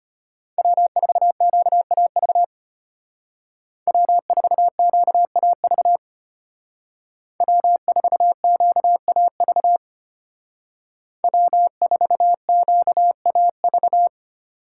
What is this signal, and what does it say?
Morse code, heard as a keyed tone, about 33 wpm, 700 Hz. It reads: W4QAV W4QAV W4QAV W4QAV